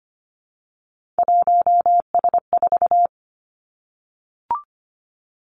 Morse code, heard as a keyed tone, about 25 wpm, 700 Hz. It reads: 1S4
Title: Three Challenging Letter-Number Combinations 25wpm